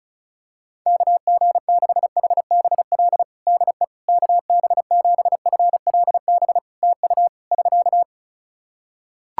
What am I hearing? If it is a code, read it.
Morse code, heard as a keyed tone, about 36 wpm, 700 Hz. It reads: KG6HBL DE KB7FLB TU <SK>
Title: Practice QSO - Exchange 4 - Ending - 35wpm